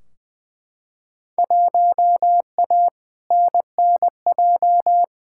Morse code, heard as a keyed tone, about 20 wpm, 700 Hz. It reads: 1A NNJ
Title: Field Day Exchanges Encore - 20wpm